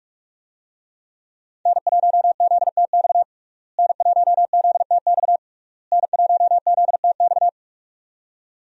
Morse code, heard as a keyed tone, about 46 wpm, 700 Hz. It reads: N1ZTX N1ZTX N1ZTX
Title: USA Call Signs - 5 Characters - Repeated 3x - 45wpm